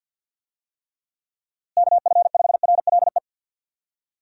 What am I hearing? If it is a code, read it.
Morse code, heard as a keyed tone, about 52 wpm, 700 Hz. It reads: KF5RLE
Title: USA Call Signs - 6 Characters - Repeated 3x - 50wpm